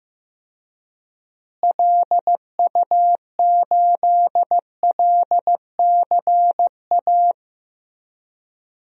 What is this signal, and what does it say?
Morse code, heard as a keyed tone, about 15 wpm, 700 Hz. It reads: LU8LCA
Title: DXCC-100 Call Signs - 6 Characters - Repeated 2x - 15wpm